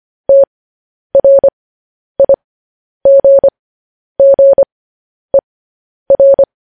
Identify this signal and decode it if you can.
Morse code, heard as a keyed tone, about 25 wpm, spaced as for 10 wpm, 550 Hz. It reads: TRIGGER
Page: news headlines in morse code at 10 wpm